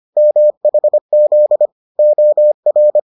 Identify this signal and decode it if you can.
Morse code, heard as a keyed tone, about 25 wpm, 600 Hz. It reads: MHZ OR